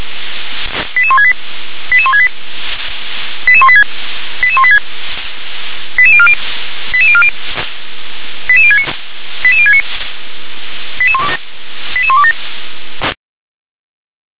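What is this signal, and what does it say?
Five-tone selective call (ZVEI), heard as quick runs of five tones, tones of about 70 ms. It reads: 88168 88168 88186 88186 88040 88040 88069 88069 88127
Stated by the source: ZVEI five-tone call 88103 received on SDR.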